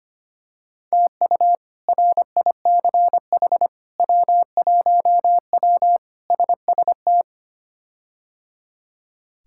Morse code, heard as a keyed tone, about 25 wpm, 700 Hz. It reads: TU RICH W1W SST
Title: K1USN SST QSO Practice - 25wpm